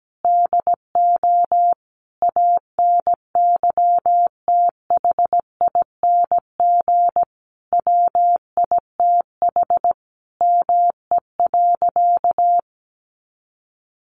Morse code, heard as a keyed tone, about 17 wpm, 700 Hz. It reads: DO ANYTHING WITH ME.
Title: Sentences from Top 1000 Words 17wpm